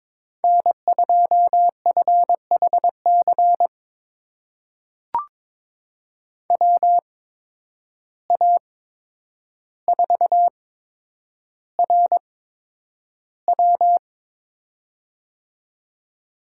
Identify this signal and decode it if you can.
Morse code, heard as a keyed tone, about 22 wpm, spaced as for 9 wpm, 700 Hz. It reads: N2FHC WA4RW
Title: USA Call Signs - 5 Characters - 8x Character Spacing - 22wpm